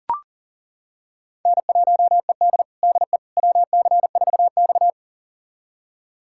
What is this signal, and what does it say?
Morse code, heard as a keyed tone, about 40 wpm, 700 Hz. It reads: N1ED DE WC4X
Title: USA Callsign DE Callsign - 4 Characters - 40wpm